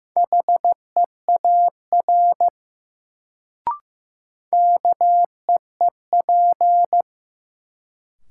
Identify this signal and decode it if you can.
Morse code, heard as a keyed tone, about 15 wpm, 700 Hz. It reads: HEAR KEEP
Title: Sentences from Top 300 Words - New Word Drill - 15wpm